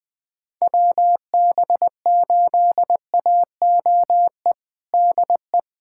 Morse code, heard as a keyed tone, about 20 wpm, 700 Hz. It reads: WB8AOE DE
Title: USA Callsign DE Callsign - 6 Characters - 20wpm